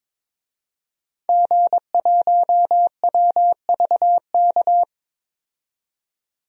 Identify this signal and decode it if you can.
Morse code, heard as a keyed tone, about 22 wpm, 700 Hz. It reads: G1WVK
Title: DXCC-100 Call Signs - 5 Characters - 22wpm